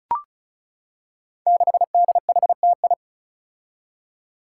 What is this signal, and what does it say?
Morse code, heard as a keyed tone, about 36 wpm, 700 Hz. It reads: 6DHTI
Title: Five Challenging Letter-Number Combinations 35wpm